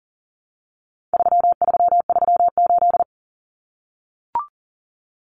Morse code, heard as a keyed tone, about 40 wpm, 700 Hz. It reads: 3338
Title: Common Cut Numbers - 4-Digit 40wpm